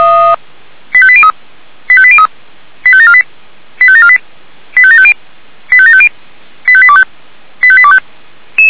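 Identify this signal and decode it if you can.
Five-tone selective call (ZVEI), heard as quick runs of five tones, tones of about 70 ms. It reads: 85803 85803 85748 85748 85750 85750 85825 85825
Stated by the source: ZVEI five-tone selective call received on an SDR.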